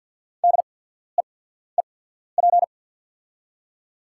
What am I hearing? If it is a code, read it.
Morse code, heard as a keyed tone, about 50 wpm, spaced as for 14 wpm, 700 Hz. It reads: DEEP
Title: 8x Character Spacing - Top 500 Words - 50wpm